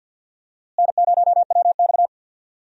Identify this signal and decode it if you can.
Morse code, heard as a keyed tone, about 50 wpm, 700 Hz. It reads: N0WX
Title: USA Call Signs - 4 Characters - Repeated 3x - 50wpm